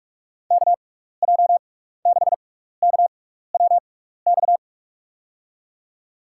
Morse code, heard as a keyed tone, about 44 wpm, spaced as for 15 wpm, 700 Hz. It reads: KJ6KWX
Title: USA Call Signs - 6 Characters - 6x Character Spacing - 45wpm